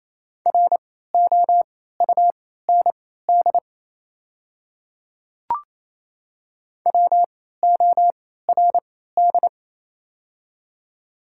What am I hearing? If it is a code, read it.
Morse code, heard as a keyed tone, about 28 wpm, spaced as for 16 wpm, 690 Hz. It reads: ROUND WORD
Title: 3x Character Spacing - Top 200 Words - 28wpm